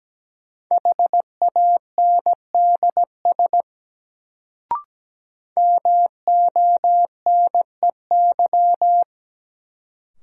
Morse code, heard as a keyed tone, about 17 wpm, 700 Hz. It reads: HANDS MONEY
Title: Sets of 4 Words - New Word Drill - 17wpm